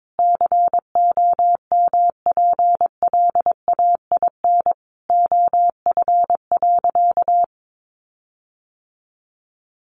Morse code, heard as a keyed tone, about 22 wpm, 700 Hz. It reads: COMPLAIN OF.
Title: Sentences from Top 1300 Words 22wpm